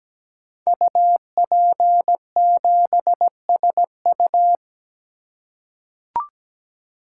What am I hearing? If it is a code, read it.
Morse code, heard as a keyed tone, about 17 wpm, 700 Hz. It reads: UP7SU
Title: Code Groups - 5 Letter:Numbers - 17wpm